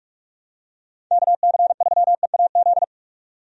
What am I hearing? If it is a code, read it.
Morse code, heard as a keyed tone, about 46 wpm, 700 Hz. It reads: KC3EAZ
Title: USA Call Signs - 6 Characters - 45wpm